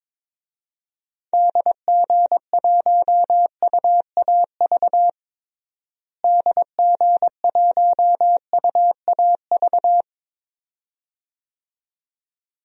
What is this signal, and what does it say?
Morse code, heard as a keyed tone, about 22 wpm, 700 Hz. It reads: DG1UAV DG1UAV
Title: DXCC-100 Call Signs - 6 Characters - Repeated 2x - 22wpm